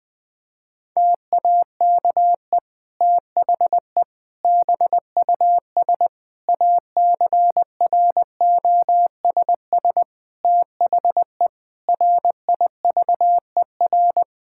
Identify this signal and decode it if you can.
Morse code, heard as a keyed tone, about 20 wpm, 700 Hz. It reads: TAKE THE BUS ACROSS THE RIVER
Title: Sentences from Top 1000 Words - Review Part 6 - 20wpm